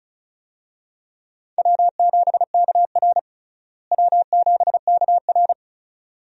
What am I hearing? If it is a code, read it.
Morse code, heard as a keyed tone, about 36 wpm, 700 Hz. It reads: W7KR W7KR
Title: USA Call Signs - 4 Characters - Repeated 2x - 35wpm